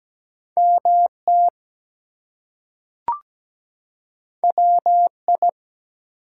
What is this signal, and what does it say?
Morse code, heard as a keyed tone, about 17 wpm, 700 Hz. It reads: MT WI
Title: US States and Canadian Provinces 17wpm